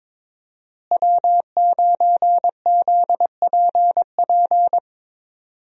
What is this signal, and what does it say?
Morse code, heard as a keyed tone, about 22 wpm, 700 Hz. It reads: W9ZPP
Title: USA Call Signs - 5 Characters - Repeated 3x - 22wpm